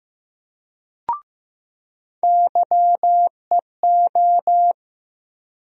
Morse code, heard as a keyed tone, about 15 wpm, 700 Hz. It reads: YEO
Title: USA Call Sign Postfixes - 3 Letters 15wpm